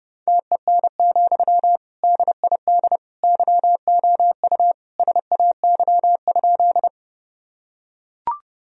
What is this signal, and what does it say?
Morse code, heard as a keyed tone, about 30 wpm, 700 Hz. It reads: TEN, DID YOU SAY?
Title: Sentences from Top 1000 Words - Review Part 2 - 30wpm